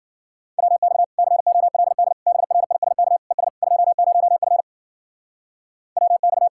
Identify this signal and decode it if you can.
Morse code, heard as a keyed tone, about 50 wpm, 700 Hz. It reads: WX COLD BRISK ES 20F WX